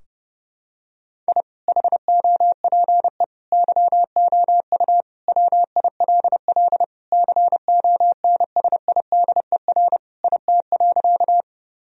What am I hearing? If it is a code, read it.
Morse code, heard as a keyed tone, about 30 wpm, 700 Hz. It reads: I HOPE YOU WILL CONSIDER IT.
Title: Sentences from Top 1000 Words - Review Part 4 - 30wpm